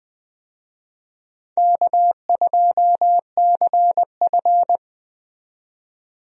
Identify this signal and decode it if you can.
Morse code, heard as a keyed tone, about 20 wpm, 700 Hz. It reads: K2CF
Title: USA Call Signs - 4 Characters - 20wpm